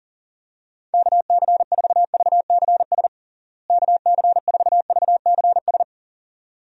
Morse code, heard as a keyed tone, about 40 wpm, 700 Hz. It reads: KC4VCS KC4VCS
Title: USA Call Signs - 6 Characters - Repeated 2x - 40wpm